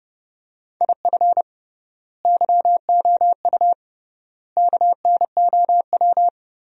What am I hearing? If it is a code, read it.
Morse code, heard as a keyed tone, about 29 wpm, 700 Hz. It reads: IF YOU KNOW
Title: 3x Word-Spacing - Sentences from Top 400 Words - 30wpm